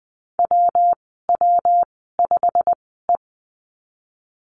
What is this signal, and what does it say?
Morse code, heard as a keyed tone, about 20 wpm, spaced as for 15 wpm, 700 Hz. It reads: WW5E
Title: 193 Lesson 31g - Letter v - Call Signs 20wpm